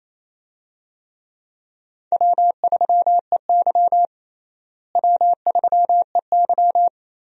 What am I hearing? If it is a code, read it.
Morse code, heard as a keyed tone, about 28 wpm, 700 Hz. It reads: W3EY W3EY
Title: USA Call Signs - 4 Characters - Repeated 2x - 28wpm